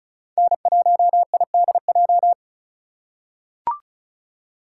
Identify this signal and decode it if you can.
Morse code, heard as a keyed tone, about 34 wpm, 700 Hz. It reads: N1IDJ
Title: USA Call Signs - 5 Characters - 4x Character Spacing - 35wpm